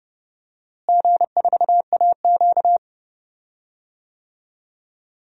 Morse code, heard as a keyed tone, about 30 wpm, 700 Hz. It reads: G4AQ
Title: DXCC-100 Call Signs - 4 Characters - Repeated 3x - 30wpm